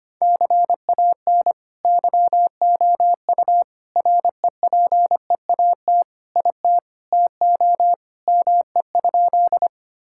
Morse code, heard as a keyed tone, about 25 wpm, 700 Hz. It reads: CAN YOU REPEAT IT TO ME?
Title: Sentences from Top 1600 Words 25wpm